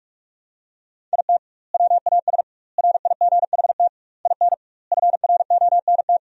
Morse code, heard as a keyed tone, about 42 wpm, 700 Hz. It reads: IT WAS RIGHT IN FRONT
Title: 2x Word-Spacing - Sentences from Top 500 Words - 45wpm